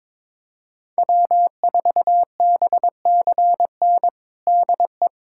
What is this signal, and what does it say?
Morse code, heard as a keyed tone, about 22 wpm, 700 Hz. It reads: W4BCN DE
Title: USA Callsign DE Callsign - 5 Characters - 22wpm